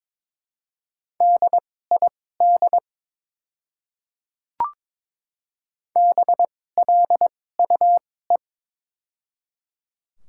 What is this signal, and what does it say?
Morse code, heard as a keyed tone, about 22 wpm, spaced as for 16 wpm, 700 Hz. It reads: DID BLUE
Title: 2x Character Spacing - Top 500 Words - 22wpm